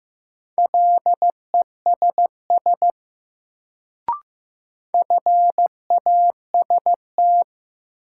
Morse code, heard as a keyed tone, about 15 wpm, 700 Hz. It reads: LESS FAST
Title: Sentences from Top 400 Words - New Word Drill - 15wpm